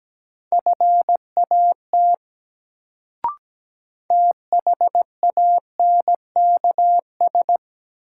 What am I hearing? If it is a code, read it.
Morse code, heard as a keyed tone, about 17 wpm, 700 Hz. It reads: FAT THANKS
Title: Sentences from Top 600 Words - New Word Drill - 17wpm